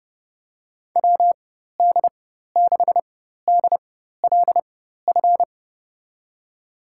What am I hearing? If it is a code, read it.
Morse code, heard as a keyed tone, about 30 wpm, spaced as for 14 wpm, 700 Hz. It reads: WD6DLF
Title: USA Call Signs - 6 Characters - 4x Character Spacing - 30wpm